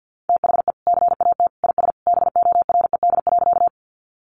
Morse code, heard as a keyed tone, about 51 wpm, 700 Hz. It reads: THE CAT IS BORED.
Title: Easy Phrases 50wpm